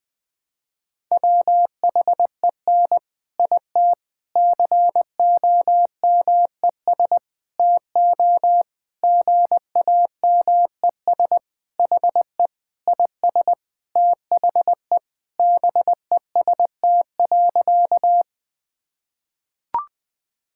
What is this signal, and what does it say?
Morse code, heard as a keyed tone, about 21 wpm, 700 Hz. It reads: WHEN IT COMES TO GAMES HE IS THE BEST.
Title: Sentences with 3-Word Phrases - 20wpm